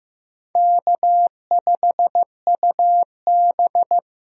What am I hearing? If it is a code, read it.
Morse code, heard as a keyed tone, about 15 wpm, 700 Hz. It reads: K5UB
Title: USA Call Signs - 4 Characters - Repeated 2x - 15wpm